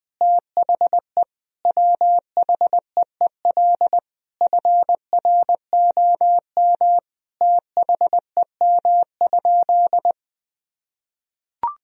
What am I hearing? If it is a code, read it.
Morse code, heard as a keyed tone, about 20 wpm, 700 Hz. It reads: THE WHEEL FROM THEM?
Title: Sentences from Top 1000 Words - Review Part 3 - 20wpm